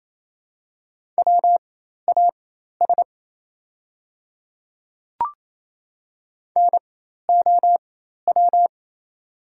Morse code, heard as a keyed tone, about 27 wpm, spaced as for 13 wpm, 690 Hz. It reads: WAS NOW
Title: 4x Character Spacing - Top 100 Words - 28wpm